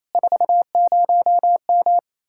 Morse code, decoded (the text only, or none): 40M